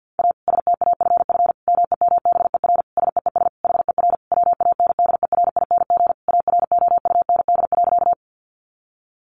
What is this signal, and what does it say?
Morse code, heard as a keyed tone, about 50 wpm, 700 Hz. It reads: A STAFF MEMBER SEES HER WANDERING AROUND.
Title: Additional Easy Phrases 50wpm